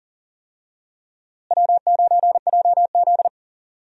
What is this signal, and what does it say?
Morse code, heard as a keyed tone, about 40 wpm, 700 Hz. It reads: W9JZ
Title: USA Call Signs - 4 Characters - Repeated 2x - 40wpm